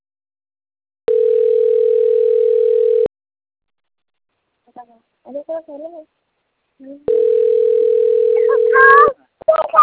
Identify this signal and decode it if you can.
Ringback tone (North America), 440 + 480 Hz, with speech.
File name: autotestagent / 11-34-35.136..callcentertransfer. .cc215eb9-0acd-4144-8e23